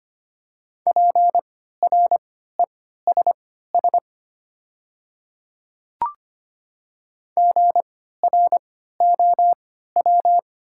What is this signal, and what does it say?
Morse code, heard as a keyed tone, about 25 wpm, spaced as for 14 wpm, 700 Hz. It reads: PRESS GROW
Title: 3x Character Spacing - Top 300 Words - 25wpm